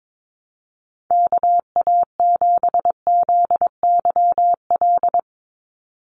Morse code, heard as a keyed tone, about 22 wpm, 700 Hz. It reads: KA7ZYL
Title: USA Call Signs 22wpm